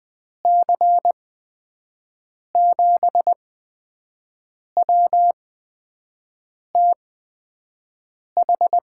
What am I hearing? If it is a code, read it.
Morse code, heard as a keyed tone, about 20 wpm, spaced as for 5 wpm, 700 Hz. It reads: C7WTH
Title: USA Call Signs - 6 Characters - 8x Character Spacing - 20wpm